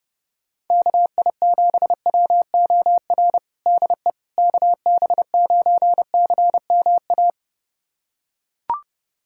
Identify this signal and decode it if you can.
Morse code, heard as a keyed tone, about 30 wpm, 700 Hz. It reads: KI7WOR DE KB9CMA
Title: USA Callsign DE Callsign - 6 Characters - 30wpm